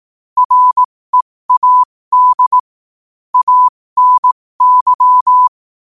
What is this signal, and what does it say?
Morse code, heard as a keyed tone, about 18 wpm, spaced as for 15 wpm, 1000 Hz. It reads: READ ANY
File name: qotdfiles / 15wpm